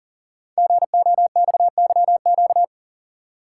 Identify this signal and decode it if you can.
Morse code, heard as a keyed tone, about 39 wpm, 700 Hz. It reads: GOXYQ
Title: Code Groups - 5 Letter:Numbers - 40wpm